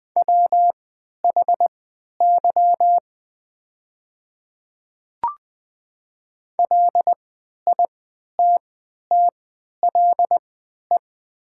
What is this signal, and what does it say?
Morse code, heard as a keyed tone, about 20 wpm, spaced as for 11 wpm, 700 Hz. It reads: WHY LITTLE
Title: 3x Character Spacing - Top 200 Words - 20wpm